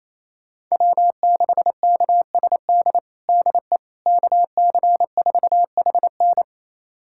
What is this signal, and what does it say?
Morse code, heard as a keyed tone, about 28 wpm, 700 Hz. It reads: W6KSD DE KC4HN
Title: USA Callsign DE Callsign - 5 Characters - 28wpm